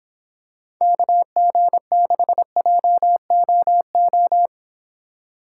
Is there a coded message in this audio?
Yes: Morse code from 0.8 to 4.5 s, heard as a keyed tone, about 26 wpm, 700 Hz. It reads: KG6JOO